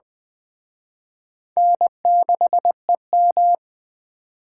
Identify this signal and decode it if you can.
Morse code, heard as a keyed tone, about 21 wpm, 700 Hz. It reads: N6EM
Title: Speed Racing - USA 1x2 Call Signs 17wpm